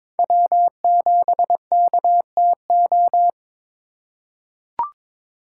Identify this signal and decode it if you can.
Morse code, heard as a keyed tone, about 22 wpm, 700 Hz. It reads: W7KTO
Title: USA Call Signs - 8x Character Spacing - 22wpm